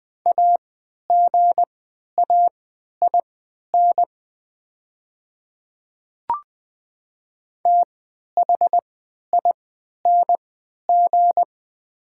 Morse code, heard as a keyed tone, about 20 wpm, spaced as for 11 wpm, 700 Hz. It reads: AGAIN THING
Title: 3x Character Spacing - Top 200 Words - 20wpm